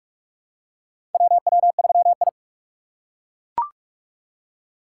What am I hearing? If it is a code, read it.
Morse code, heard as a keyed tone, about 42 wpm, 700 Hz. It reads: WW3I
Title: USA Call Signs - 4 Characters - 6x Character Spacing - 45wpm